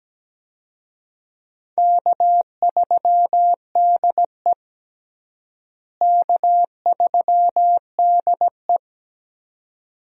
Morse code, heard as a keyed tone, about 17 wpm, 700 Hz. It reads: K3DE K3DE
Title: USA Call Signs - 4 Characters - Repeated 2x - 17wpm